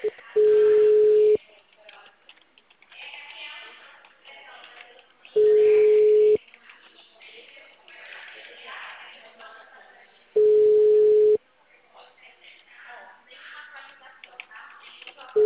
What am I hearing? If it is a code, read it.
Ringback tone (Europe, 425 Hz), with speech.